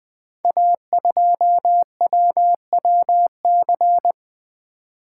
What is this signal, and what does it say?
Morse code, heard as a keyed tone, about 20 wpm, 700 Hz. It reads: A2WWC
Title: Code Groups - 5 Letter:Numbers - 20wpm